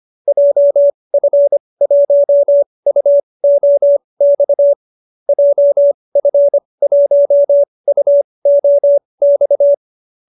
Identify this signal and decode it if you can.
Morse code, heard as a keyed tone, about 25 wpm, spaced as for 20 wpm, 590 Hz. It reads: JF1UOX JF1UOX